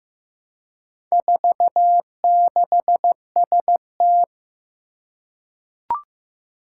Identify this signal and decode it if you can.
Morse code, heard as a keyed tone, about 15 wpm, 700 Hz. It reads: 46ST
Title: Four Challenging Letter-Number Combinations 15wpm